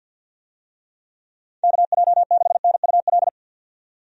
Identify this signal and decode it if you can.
Morse code, heard as a keyed tone, about 51 wpm, 700 Hz. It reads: KJ6NUL
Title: USA Call Signs - 6 Characters - 50wpm